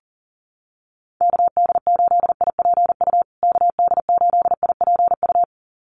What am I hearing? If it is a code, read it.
Morse code, heard as a keyed tone, about 40 wpm, 700 Hz. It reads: KD8IPU KD8IPU
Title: USA Call Signs Repeated 2x 40wpm